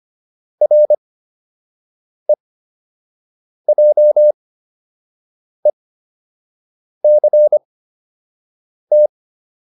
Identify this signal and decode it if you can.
Morse code, heard as a keyed tone, about 25 wpm, spaced as for 6 wpm, 600 Hz. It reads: REJECT